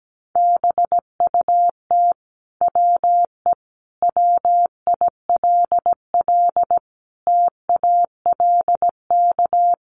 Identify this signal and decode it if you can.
Morse code, heard as a keyed tone, about 17 wpm, 700 Hz. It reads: BUT WE WILL TALK
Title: Sentences from Top 1200 Words 17wpm